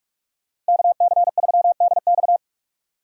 Morse code, heard as a keyed tone, about 42 wpm, 700 Hz. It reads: KC3DX
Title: USA Call Signs - 4x Character Spacing - 45wpm